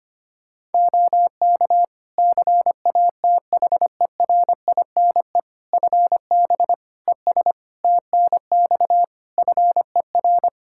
Morse code, heard as a keyed tone, about 25 wpm, 700 Hz. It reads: OK CATHERINE FB ES TNX FER
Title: Practice QSO - Exchange 2 - Info - 25wpm